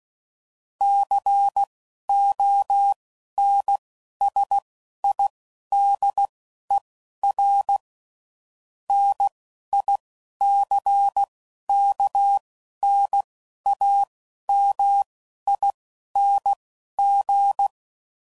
Morse code, heard as a keyed tone, about 16 wpm, spaced as for 11 wpm, 800 Hz. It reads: CONSIDER NICKNAMING
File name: CW Code Groups @10WPM/Technical Subjects